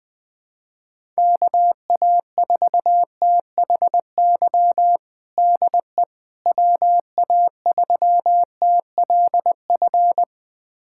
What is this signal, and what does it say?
Morse code, heard as a keyed tone, about 20 wpm, 700 Hz. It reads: KA4THY DE WA3TLF
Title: USA Callsign DE Callsign 20wpm